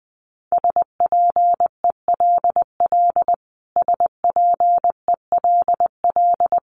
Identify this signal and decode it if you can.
Morse code, heard as a keyed tone, about 20 wpm, 700 Hz. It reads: SPELL SPELL
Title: Top 300 Words Repeated 20wpm